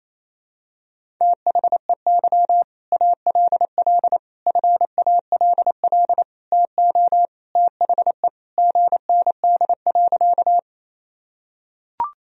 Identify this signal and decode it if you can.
Morse code, heard as a keyed tone, about 28 wpm, 700 Hz. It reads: THEY ALL FALL TO THE GND.
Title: Sentences from Top 500 Words and Abbreviations 28wpm